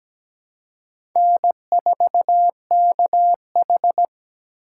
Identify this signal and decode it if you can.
Morse code, heard as a keyed tone, about 17 wpm, 700 Hz. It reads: N4KH
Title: K1USN SST QSO Practice - 17wpm